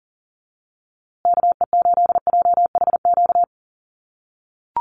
Morse code, heard as a keyed tone, about 40 wpm, 700 Hz. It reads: KE8JHQ
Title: USA Call Signs Repeated 2x 40wpm